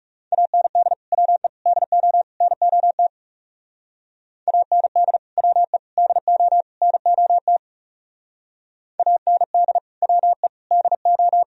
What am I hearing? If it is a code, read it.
Morse code, heard as a keyed tone, about 36 wpm, 700 Hz. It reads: AND WE DO NOT AND WE DO NOT AND WE DO
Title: Speed-Racing - Sets of 4 Words 30wpm